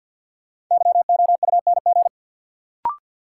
Morse code, heard as a keyed tone, about 51 wpm, 700 Hz. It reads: YOUNG
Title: Sentences from Top 400 Words - New Word Drill - 50wpm